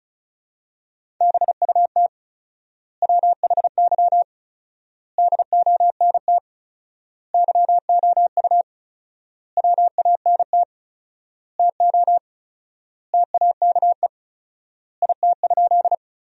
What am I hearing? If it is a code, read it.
Morse code, heard as a keyed tone, about 34 wpm, 700 Hz. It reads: BUT WHY DONT YOU WANT TO TAKE IT?
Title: 4x Word-Spacing - Sentences from Top 300 Words - 35wpm